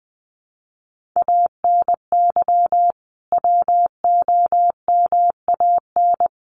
Morse code, heard as a keyed tone, about 20 wpm, 700 Hz. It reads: ANY WOMAN
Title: Sentences from Top 1100 Words 20wpm